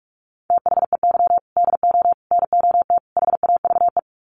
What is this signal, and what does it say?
Morse code, heard as a keyed tone, about 44 wpm, 700 Hz. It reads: THEY DO NOT HAVE
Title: Sets of 4 Words 45wpm